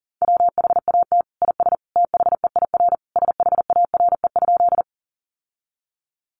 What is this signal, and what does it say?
Morse code, heard as a keyed tone, about 40 wpm, 700 Hz. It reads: WHAT IS THEIR SHARE?
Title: Sentences from Top 900 Words 40wpm